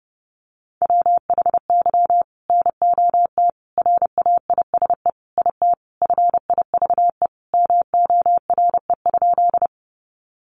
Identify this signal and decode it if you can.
Morse code, heard as a keyed tone, about 30 wpm, 700 Hz. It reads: WHY NOT RAISE IT FIVE MORE?
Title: Sentences from Top 1400 Words 30wpm